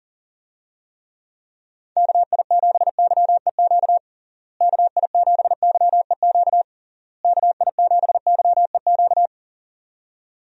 Morse code, heard as a keyed tone, about 40 wpm, 700 Hz. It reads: KI7YEQ KI7YEQ KI7YEQ
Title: USA Call Signs - 6 Characters - Repeated 3x - 40wpm